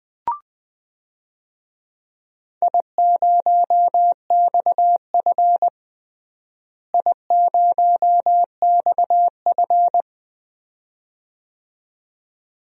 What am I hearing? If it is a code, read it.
Morse code, heard as a keyed tone, about 20 wpm, 700 Hz. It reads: I0XF I0XF
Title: DXCC-100 Call Signs - 4 Characters - Repeated 2x - 20wpm